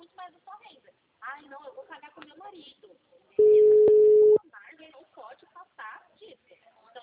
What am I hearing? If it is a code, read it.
Ringback tone (Europe, 425 Hz), with speech.
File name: out- -2011- - - .31092